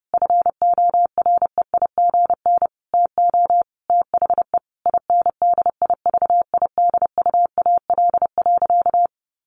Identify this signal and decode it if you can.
Morse code, heard as a keyed tone, about 30 wpm, 700 Hz. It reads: FOREIGN TO THE INDIVIDUAL.
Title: Sentences from Top 1400 Words 30wpm